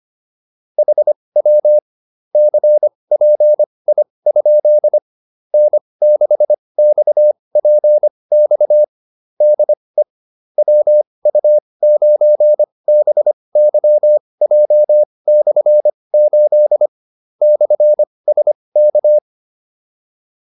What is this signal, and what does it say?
Morse code, heard as a keyed tone, about 25 wpm, spaced as for 20 wpm, 600 Hz. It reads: HW CPI? N6XPX DE WU9BYJ/8 /SK